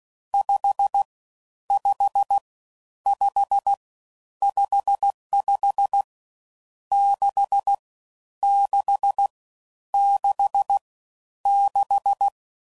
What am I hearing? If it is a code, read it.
Morse code, heard as a keyed tone, about 16 wpm, 800 Hz. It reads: 5 5 5 55 6 6 6 6
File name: Mixed / Misc / Numbs